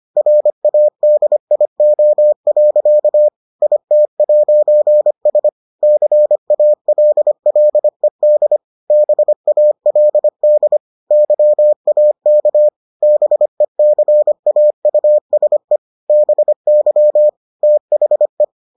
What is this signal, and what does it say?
Morse code, heard as a keyed tone, about 25 wpm, 600 Hz. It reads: RADIO. IT'S CALLED BALD YAK BECAUSE BY THE